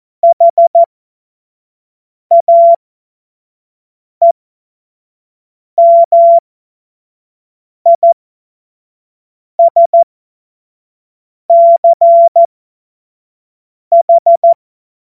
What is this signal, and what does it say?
Morse code, heard as a keyed tone, about 14 wpm, spaced as for 5 wpm, 680 Hz. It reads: HAEMISCH
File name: Morsekurs / Witze / Witz 04. Oktober 2022 25 F70